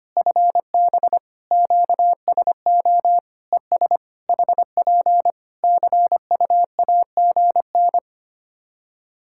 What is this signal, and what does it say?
Morse code, heard as a keyed tone, about 25 wpm, 700 Hz. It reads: FB QSO ES HP CUAGN
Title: K1USN SST QSO Practice 25wpm